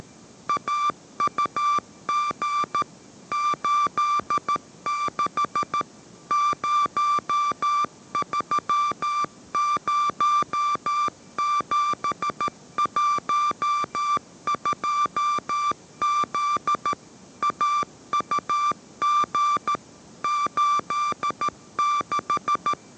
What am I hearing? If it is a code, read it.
Morse code, heard as a keyed tone, about 13 wpm, 1180 Hz. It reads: AUG 86 030712Z AUG 86
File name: 14wpm